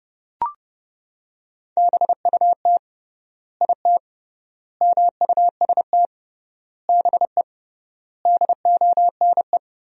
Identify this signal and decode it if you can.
Morse code, heard as a keyed tone, about 32 wpm, 700 Hz. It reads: BUT IT MUST BE DONE
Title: 3x Word-Spacing - Sentences from Top 500 Words - 30wpm